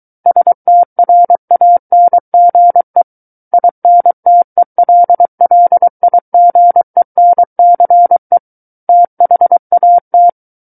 Morse code, heard as a keyed tone, about 23 wpm, 700 Hz. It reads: STRANGE INTELLIGENCE THAT
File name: Book One / 14-Chapter 14 - 23 WPM